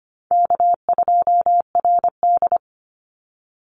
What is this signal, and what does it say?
Morse code, heard as a keyed tone, about 25 wpm, 700 Hz. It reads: K2RD
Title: Easy Call Signs Repeated 2x - 25wpm